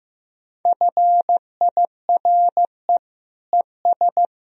Morse code, heard as a keyed tone, about 15 wpm, 700 Hz. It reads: FIRE ES
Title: Sentences from Top 400 Words and Abbreviations 15wpm